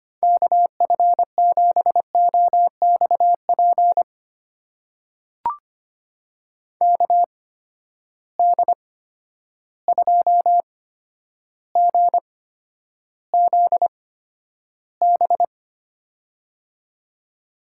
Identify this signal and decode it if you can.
Morse code, heard as a keyed tone, about 25 wpm, spaced as for 11 wpm, 700 Hz. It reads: KF7OXP KD2GZB
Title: USA Call Signs - 8x Character Spacing - 25wpm